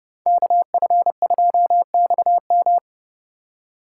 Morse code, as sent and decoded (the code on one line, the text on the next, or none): -.- ..-. ..--- -..- --
KF2XM